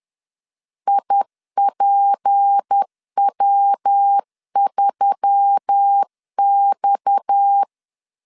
Morse code, heard as a keyed tone, about 10 wpm, 800 Hz. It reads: IPW3X